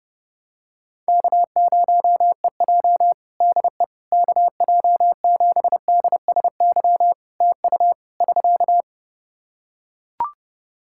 Morse code, heard as a keyed tone, about 30 wpm, 700 Hz. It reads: K0EJ DE KJ7DSY TU <SK>
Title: Practice QSO - Exchange 4 - Ending - 30wpm